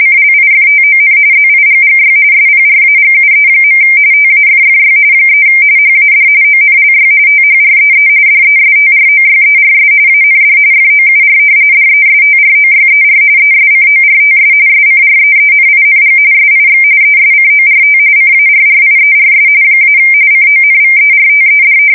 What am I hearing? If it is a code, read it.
RTTY, about 45.45 baud. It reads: CLIFTON ZIGGY SPRAGUE. TAFFY 3 HAD ONLY THREE DESTROYERS AND FOUR DESTROYER ESCORTS SCREENING SIX ESCORT CARRIERS - CONVERTED
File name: fsk170 / battleshipnj